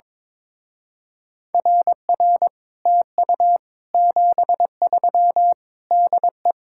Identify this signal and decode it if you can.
Morse code, heard as a keyed tone, about 22 wpm, 700 Hz. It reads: RR TU 73 DE
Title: Speed-Racing Common QSO Phrases Encore - 22wpm